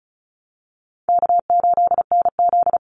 Morse code, heard as a keyed tone, about 34 wpm, 700 Hz. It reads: K8NZ
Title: Easy Call Signs Repeated 3x - 35wpm